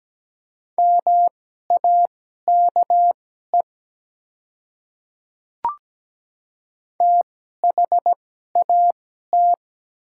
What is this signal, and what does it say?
Morse code, heard as a keyed tone, about 17 wpm, spaced as for 12 wpm, 700 Hz. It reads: MAKE THAT
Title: 2x Character Spacing - Top 100 Words - 17wpm